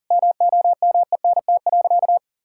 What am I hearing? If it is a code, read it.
Morse code, heard as a keyed tone, about 40 wpm, 700 Hz. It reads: MOMENT.
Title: Sentences from Top 1000 Words - Review Part 6 - 40wpm